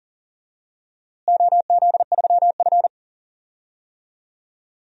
Morse code, heard as a keyed tone, about 40 wpm, 700 Hz. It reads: OZ3F
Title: DXCC-100 Call Signs - 4 Characters - Repeated 3x - 40wpm